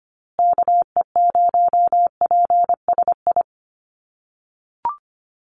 Morse code, heard as a keyed tone, about 25 wpm, 700 Hz. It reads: KE0PSI
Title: USA Call Signs Repeated 2x 25wpm